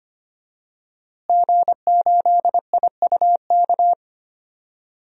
Morse code, heard as a keyed tone, about 25 wpm, 700 Hz. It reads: G8IUK
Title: DXCC-100 Call Signs - 5 Characters - Repeated 3x - 25wpm